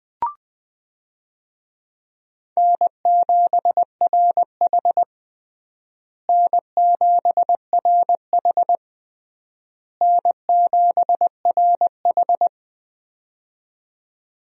Morse code, heard as a keyed tone, about 20 wpm, 700 Hz. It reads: N7RH N7RH N7RH
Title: USA Call Signs - 4 Characters - Repeated 3x - 20wpm